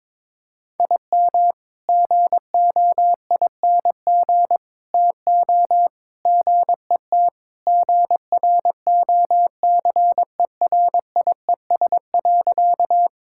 Morse code, heard as a keyed tone, about 22 wpm, 700 Hz. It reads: IM GOING TO GET GROCERIES.
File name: practice / 22wpm / Sentences with 3-Word Phrases - 22wpm